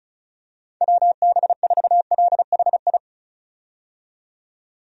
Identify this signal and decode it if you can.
Morse code, heard as a keyed tone, about 36 wpm, 700 Hz. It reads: WB4LHI
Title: USA Call Signs - 6 Characters - 35wpm